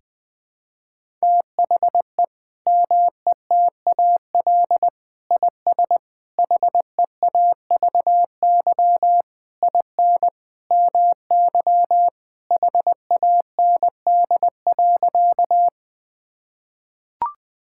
Morse code, heard as a keyed tone, about 20 wpm, 700 Hz. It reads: THE METAL IS HEAVY IN MY HAND.
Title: Sentences from Top 1000 Words - Review Part 6 - 20wpm